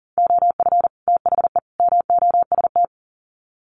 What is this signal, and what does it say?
Morse code, heard as a keyed tone, about 40 wpm, 700 Hz. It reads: OF THE MOST
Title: Sets of 3 Words 40wpm